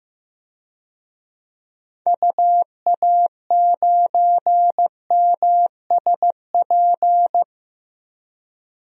Morse code, heard as a keyed tone, about 15 wpm, 700 Hz. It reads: UA9MSP
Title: DXCC-100 Call Signs - 6 Characters - 15wpm